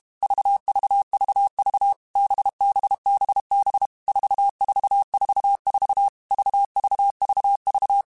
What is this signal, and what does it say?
Morse code, heard as a keyed tone, about 32 wpm, 800 Hz. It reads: VVVV BBBB 4444 VVVV
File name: Groups@30 / Groups@30 mp3s / VB46@30